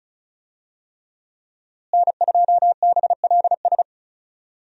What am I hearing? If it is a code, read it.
Morse code, heard as a keyed tone, about 34 wpm, 700 Hz. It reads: N2BLS
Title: USA Call Signs - 5 Characters - Repeated 3x - 35wpm